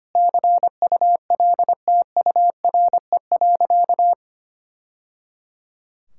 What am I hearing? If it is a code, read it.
Morse code, heard as a keyed tone, about 25 wpm, 700 Hz. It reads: CULTURE.
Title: Sentences from Top 1000 Words - Review Part 6 - 25wpm